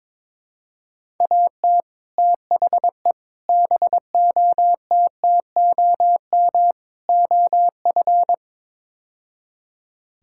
Morse code, heard as a keyed tone, about 22 wpm, 700 Hz. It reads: AT THE BOTTOM OF
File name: practice / 22wpm / Build A Phrase - 5-Word Phrases - 22wpm